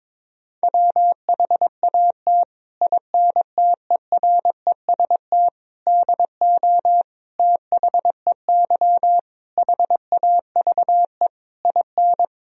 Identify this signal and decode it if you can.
Morse code, heard as a keyed tone, about 22 wpm, 700 Hz. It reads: WHAT INTEREST DO THEY HAVE IN
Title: Sentences from Top 1000 Words - Review Part 2 - 22wpm